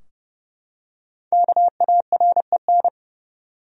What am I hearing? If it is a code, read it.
Morse code, heard as a keyed tone, about 30 wpm, 700 Hz. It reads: KAREN
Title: USA Common Names - Max 5 Letters - 30wpm